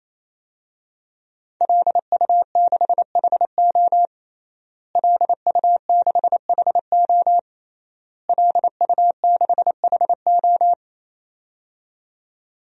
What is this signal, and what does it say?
Morse code, heard as a keyed tone, about 27 wpm, 700 Hz. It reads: LU6HO LU6HO LU6HO
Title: DXCC-100 Call Signs - 5 Characters - Repeated 3x - 28wpm